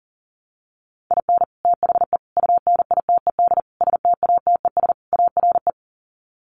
Morse code, heard as a keyed tone, about 40 wpm, 700 Hz. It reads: IN THE UNITED STATES ARE
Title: Sets of 5 Words Encore 40wpm